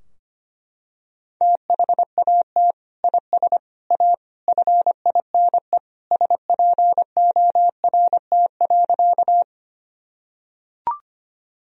Morse code, heard as a keyed tone, about 25 wpm, 700 Hz. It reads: THAT IS A FINE SPORT.
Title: Sentences from Top 1000 Words - Review Part 4 - 25wpm